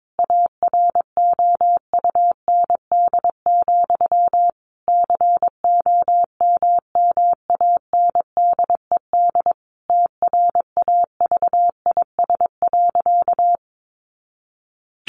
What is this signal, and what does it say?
Morse code, heard as a keyed tone, about 22 wpm, 700 Hz. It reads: AROUND, COMMANDED TRAVIS.
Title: A Sound Of Thunder 22wpm